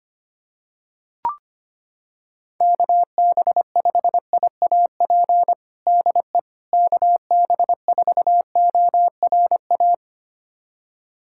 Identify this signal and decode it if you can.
Morse code, heard as a keyed tone, about 25 wpm, 700 Hz. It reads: KB5IAP DE KB4ORA
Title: USA Callsign DE Callsign - 6 Characters - 25wpm